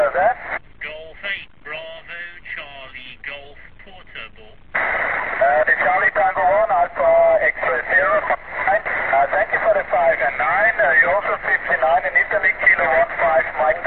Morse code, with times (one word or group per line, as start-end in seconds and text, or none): none